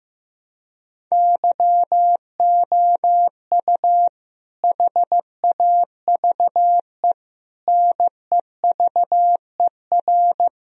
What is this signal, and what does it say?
Morse code, heard as a keyed tone, about 15 wpm, 700 Hz. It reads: YOU HAVE NEVER
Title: Sentences from Top 1000 Words - Review Part 2 - 15wpm